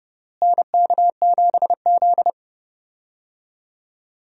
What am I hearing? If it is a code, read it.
Morse code, heard as a keyed tone, about 30 wpm, 700 Hz. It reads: NK7Z
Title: USA Call Signs - 4 Characters - Repeated 3x - 30wpm